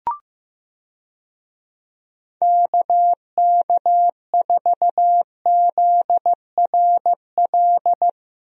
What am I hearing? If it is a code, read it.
Morse code, heard as a keyed tone, about 15 wpm, 700 Hz. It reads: KK4ZRL
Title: USA Call Signs - 6 Characters - Repeated 2x - 15wpm